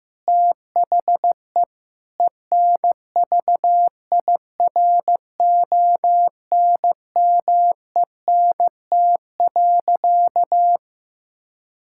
Morse code, heard as a keyed tone, about 15 wpm, 700 Hz. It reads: THE ENVIRONMENT.
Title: Sentences from Top 1000 Words - Review Part 6 - 15wpm